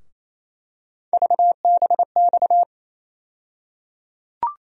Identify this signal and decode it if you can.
Morse code, heard as a keyed tone, about 28 wpm, 700 Hz. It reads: VBX
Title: USA Call Sign Postfixes - 3 Letters 28wpm